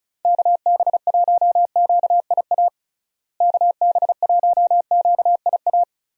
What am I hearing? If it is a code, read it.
Morse code, heard as a keyed tone, about 34 wpm, 700 Hz. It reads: KB1QIA KB1QIA
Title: USA Call Signs - 6 Characters - Repeated 3x - 35wpm